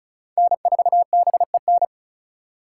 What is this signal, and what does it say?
Morse code, heard as a keyed tone, about 36 wpm, 700 Hz. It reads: N4BEN
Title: USA Call Signs - 5 Characters - 4x Character Spacing - 35wpm